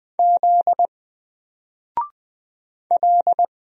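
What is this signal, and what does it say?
Morse code, heard as a keyed tone, about 20 wpm, 700 Hz. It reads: Z L
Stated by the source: Single letter and number Morse drill at 20 wpm, from N4LCD.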